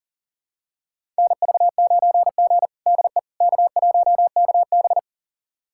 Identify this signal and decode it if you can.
Morse code, heard as a keyed tone, about 40 wpm, 700 Hz. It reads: NV9G DE K1KB
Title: USA Callsign DE Callsign - 4 Characters - 40wpm